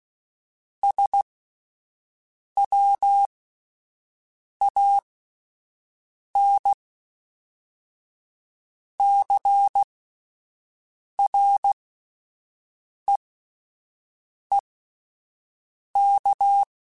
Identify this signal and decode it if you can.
Morse code, heard as a keyed tone, about 16 wpm, spaced as for 5 wpm, 800 Hz. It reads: SWAN CREEK